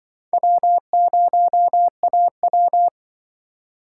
Morse code, heard as a keyed tone, about 25 wpm, 700 Hz. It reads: W0AW